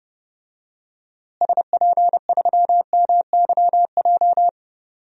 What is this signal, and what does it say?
Morse code, heard as a keyed tone, about 30 wpm, 700 Hz. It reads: SP3MYJ
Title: DXCC-100 Call Signs - 6 Characters - Repeated 3x - 30wpm